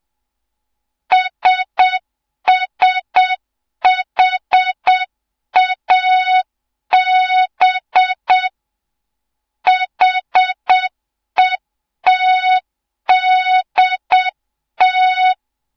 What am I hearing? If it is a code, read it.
Morse code, heard as a keyed tone, about 7 wpm, 3700 Hz. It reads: SSHAB HETDT
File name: Morze-HA2MN / hang / 07 GYAKORLAT